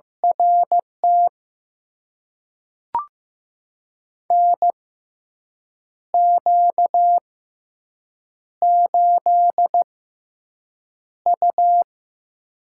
Morse code, heard as a keyed tone, about 15 wpm, spaced as for 6 wpm, 700 Hz. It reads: RT NQ8U
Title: USA Call Signs - 6x Character Spacing - 15wpm